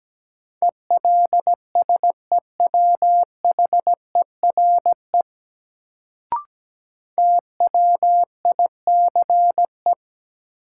Morse code, heard as a keyed tone, about 17 wpm, 700 Hz. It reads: ELSEWHERE TWICE
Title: Sentences from Top 1600 Words - New Word Drill - 17wpm